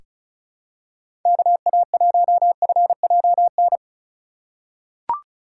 Morse code, heard as a keyed tone, about 36 wpm, 700 Hz. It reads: KA1FJN
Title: USA Call Signs - 6 Characters - 2x Character Spacing - 35wpm